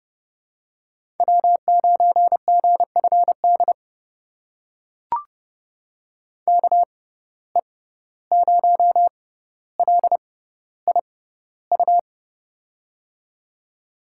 Morse code, heard as a keyed tone, about 30 wpm, spaced as for 15 wpm, 700 Hz. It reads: W9GFD KE0LIU
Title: USA Call Signs - 6x Character Spacing - 30wpm